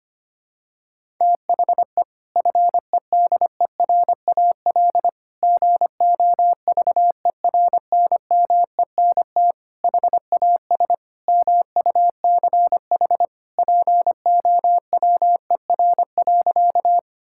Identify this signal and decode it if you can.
Morse code, heard as a keyed tone, about 25 wpm, 700 Hz. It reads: THE FEDERAL GOVERNMENT HAS MUCH POWER.
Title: Sentences from Top 1000 Words - Review Part 5 - 25wpm